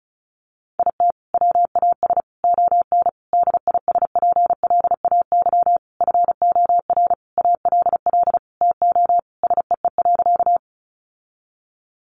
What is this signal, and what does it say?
Morse code, heard as a keyed tone, about 34 wpm, 700 Hz. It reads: IT WAS ON DISPLAY FOR ALL TO SEE.
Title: Sentences from Top 1200 Words 35wpm